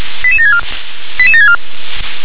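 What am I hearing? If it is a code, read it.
Five-tone selective call (ZVEI), heard as quick runs of five tones, tones of about 70 ms. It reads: 88764 88764
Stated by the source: ZVEI five-tone call received on SDR.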